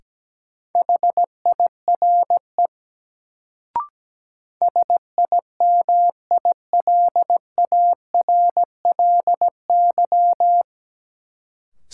Morse code, heard as a keyed tone, about 17 wpm, 700 Hz. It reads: HIRE SIMILARLY
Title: Sentences from Top 1400 Words - New Word Drill - 17wpm